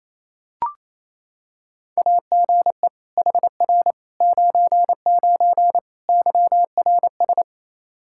Morse code, heard as a keyed tone, about 27 wpm, 700 Hz. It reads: AGE HR 99 YRS
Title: Practice QSO - Exchange 3 - More Info - 28wpm